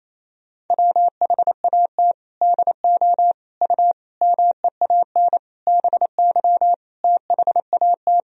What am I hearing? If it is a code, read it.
Morse code, heard as a keyed tone, about 28 wpm, 700 Hz. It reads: WHAT DO U MEAN BY THAT
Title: Sentences from Top 400 Words and Abbreviations 28wpm